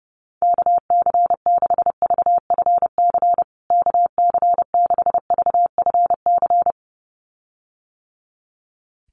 Morse code, heard as a keyed tone, about 30 wpm, 700 Hz. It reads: KC6VFC KC6VFC
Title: USA Call Signs Repeated 2x 30wpm